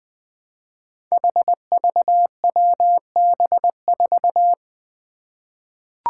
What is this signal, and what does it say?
Morse code, heard as a keyed tone, about 20 wpm, 700 Hz. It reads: HVWB4
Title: Five Challenging Letter-Number Combinations 20wpm